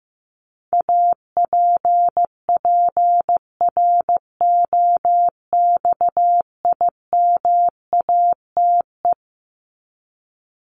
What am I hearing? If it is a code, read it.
Morse code, heard as a keyed tone, about 15 wpm, 700 Hz. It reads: APPROXIMATE
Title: LY Suffix - Top 100 Words with Base Words 15wpm